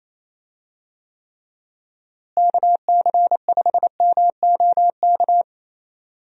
Morse code, heard as a keyed tone, about 27 wpm, 700 Hz. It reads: KC5MOK
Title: USA Call Signs - 6 Characters - Repeated 3x - 28wpm